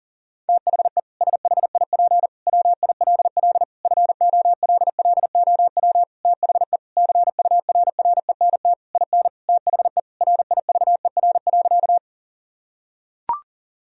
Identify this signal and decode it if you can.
Morse code, heard as a keyed tone, about 40 wpm, 700 Hz. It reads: THE SHIP WILL FOLLOW THE CURRENT IN THE RIVER.
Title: Sentences from Top 1000 Words - Review Part 4 - 40wpm